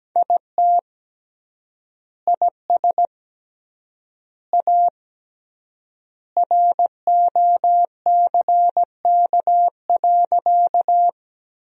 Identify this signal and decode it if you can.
Morse code, heard as a keyed tone, about 17 wpm, 700 Hz. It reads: IT IS A ROCK.
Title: 3x Word-Spacing - Sentences from Top 400 Words - 17wpm